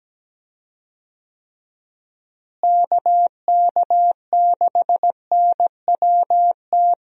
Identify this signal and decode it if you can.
Morse code, heard as a keyed tone, about 17 wpm, 700 Hz. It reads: KK6NWT
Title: USA Call Signs - 6 Characters - Repeated 2x - 17wpm